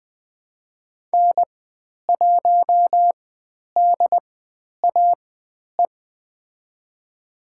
Morse code, heard as a keyed tone, about 20 wpm, spaced as for 10 wpm, 700 Hz. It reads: N1DAE
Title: 095 Lesson 16g - Letter c - Call Signs 20wpm